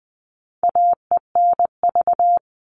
Morse code, heard as a keyed tone, about 20 wpm, 700 Hz. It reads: AENV
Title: All Cut Numbers - 4-Digit 20wpm